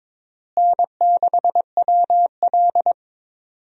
Morse code, heard as a keyed tone, about 22 wpm, 700 Hz. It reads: N6WL
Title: USA Call Signs - 4 Characters - Repeated 3x - 22wpm